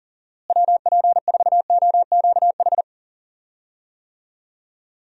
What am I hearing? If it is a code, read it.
Morse code, heard as a keyed tone, about 40 wpm, 700 Hz. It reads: WP4OQH
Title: USA Call Signs - 6 Characters - Repeated 2x - 40wpm